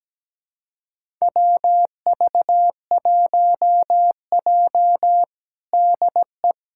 Morse code, heard as a keyed tone, about 17 wpm, 700 Hz. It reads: WV1J DE
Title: USA Callsign DE Callsign - 4 Characters - 17wpm